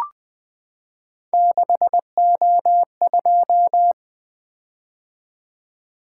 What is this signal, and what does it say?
Morse code, heard as a keyed tone, about 20 wpm, 700 Hz. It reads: 6O2
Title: Three Letter-Number Combinations - 20wpm